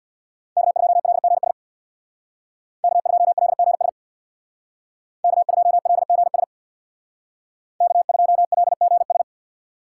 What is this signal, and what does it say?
Morse code, heard as a keyed tone, about 59 wpm, 700 Hz. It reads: K2LGS K2LGS K2LGS K2LGS
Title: Speed Racing - USA 1x3 Call Signs 50wpm